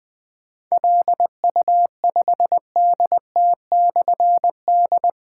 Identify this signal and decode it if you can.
Morse code, heard as a keyed tone, about 20 wpm, 700 Hz. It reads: LU5DT/D
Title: Difficult Call Signs 20wpm